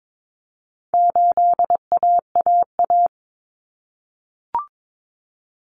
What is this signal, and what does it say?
Morse code, heard as a keyed tone, about 22 wpm, 700 Hz. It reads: 8AAA
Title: Common Cut Numbers - 4-Digit 22wpm